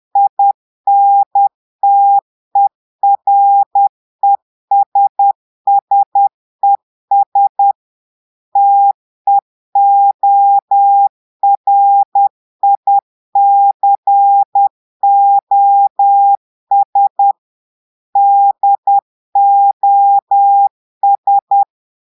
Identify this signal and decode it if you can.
Morse code, heard as a keyed tone, about 10 wpm, 800 Hz. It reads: INTERESSES TEORICOS DOS
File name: Portugais - texte en clair 1 - vitesse - 10 mots minute